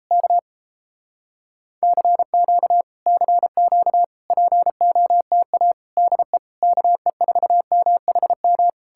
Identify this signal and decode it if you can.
Morse code, heard as a keyed tone, about 38 wpm, 700 Hz. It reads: K CQ CQ POTA DE KE4MHM